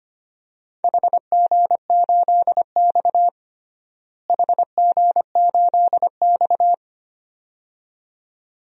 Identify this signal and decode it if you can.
Morse code, heard as a keyed tone, about 25 wpm, 700 Hz. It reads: HG8X HG8X
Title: DXCC-100 Call Signs - 4 Characters - Repeated 2x - 25wpm